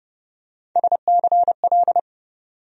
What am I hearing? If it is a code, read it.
Morse code, heard as a keyed tone, about 30 wpm, 700 Hz. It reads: SCL
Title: USA Call Sign Postfixes - 3 Letters 30wpm